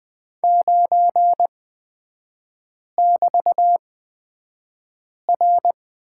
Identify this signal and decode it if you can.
Morse code, heard as a keyed tone, about 20 wpm, spaced as for 5 wpm, 700 Hz. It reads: 9<BT>R
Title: 250 Lesson 40e - BK - Three Characters 20wpm